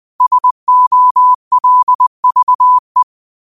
Morse code, heard as a keyed tone, about 20 wpm, 1000 Hz. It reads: SOLVE
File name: qotdfiles / 20wpm